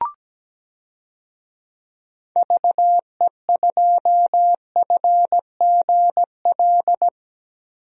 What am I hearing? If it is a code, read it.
Morse code, heard as a keyed tone, about 17 wpm, 700 Hz. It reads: VE2FGL
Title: DXCC-100 Call Signs - 6 Characters - Repeated 3x - 17wpm